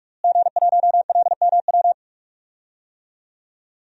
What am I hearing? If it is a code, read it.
Morse code, heard as a keyed tone, about 46 wpm, 700 Hz. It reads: G1LMW
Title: DXCC-100 Call Signs - 5 Characters - Repeated 2x - 45wpm